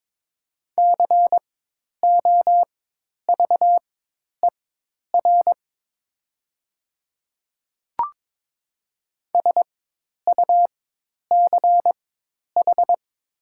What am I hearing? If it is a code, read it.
Morse code, heard as a keyed tone, about 22 wpm, spaced as for 10 wpm, 700 Hz. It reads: COVER SUCH
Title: 4x Character Spacing - Top 300 Words - 22wpm